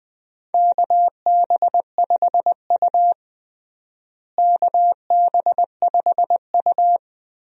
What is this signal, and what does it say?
Morse code, heard as a keyed tone, about 21 wpm, 700 Hz. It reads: KB5U KB5U
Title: USA Call Signs - 4 Characters - Repeated 3x - 20wpm